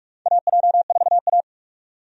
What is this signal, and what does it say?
Morse code, heard as a keyed tone, about 46 wpm, 700 Hz. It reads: AJ4A
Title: USA Call Signs - 4 Characters - 45wpm